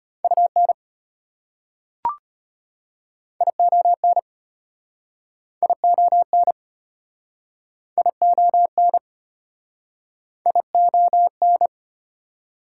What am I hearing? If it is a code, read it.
Morse code, heard as a keyed tone, about 34 wpm, 700 Hz. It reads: UN ION ION ION ION